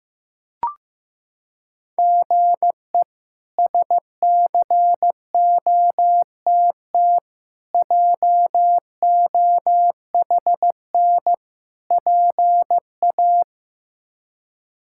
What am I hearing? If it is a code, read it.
Morse code, heard as a keyed tone, about 15 wpm, 700 Hz. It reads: GE SCOTT JOHN PA
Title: K1USN SST QSO Practice - 15wpm